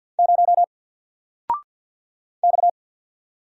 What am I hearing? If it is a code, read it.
Morse code, heard as a keyed tone, about 50 wpm, 700 Hz. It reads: 0 X
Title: Single Letter-Number Alt 50wpm